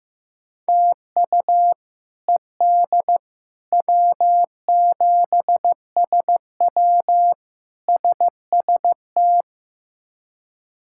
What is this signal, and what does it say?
Morse code, heard as a keyed tone, about 15 wpm, 700 Hz. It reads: TU ED W7SW SST
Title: K1USN SST QSO Practice - 15wpm